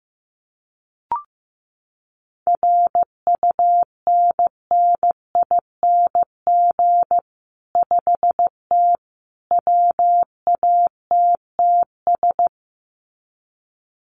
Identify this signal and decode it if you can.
Morse code, heard as a keyed tone, about 15 wpm, 700 Hz. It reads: RUNNING 5T WATTS
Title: Common QSO Phrases 15wpm